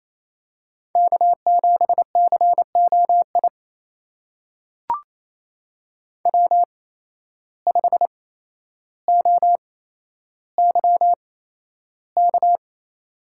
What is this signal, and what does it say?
Morse code, heard as a keyed tone, about 27 wpm, spaced as for 12 wpm, 700 Hz. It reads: K7COI W5OYK
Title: USA Call Signs - 5 Characters - 8x Character Spacing - 28wpm